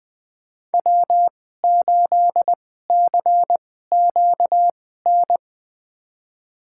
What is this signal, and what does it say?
Morse code, heard as a keyed tone, about 20 wpm, spaced as for 14 wpm, 700 Hz. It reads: W8CQN
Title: USA Call Signs - 2x Character Spacing - 20wpm